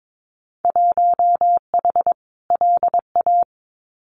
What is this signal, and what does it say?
Morse code, heard as a keyed tone, about 22 wpm, 700 Hz. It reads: 1H LA
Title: Winter Field Day Exchanges 22wpm